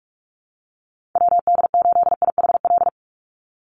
Morse code, heard as a keyed tone, about 44 wpm, 700 Hz. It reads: WD8IHL
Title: USA Call Signs 45wpm